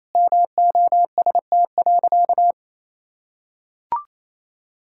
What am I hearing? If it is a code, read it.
Morse code, heard as a keyed tone, about 28 wpm, 700 Hz. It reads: MOST.
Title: Sentences from Top 1600 Words 28wpm